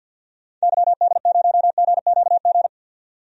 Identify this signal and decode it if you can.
Morse code, heard as a keyed tone, about 52 wpm, 700 Hz. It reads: YD0GQG
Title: DXCC-100 Call Signs - 6 Characters - 50wpm